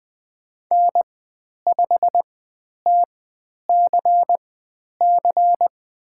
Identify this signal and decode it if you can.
Morse code, heard as a keyed tone, about 20 wpm, spaced as for 10 wpm, 700 Hz. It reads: N5TCC
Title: 102 Lesson 17g - Letter u - Call Signs 20wpm